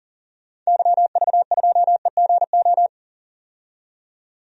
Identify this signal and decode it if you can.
Morse code, heard as a keyed tone, about 40 wpm, 700 Hz. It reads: YV2EGO
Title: DXCC-100 Call Signs - 6 Characters - 40wpm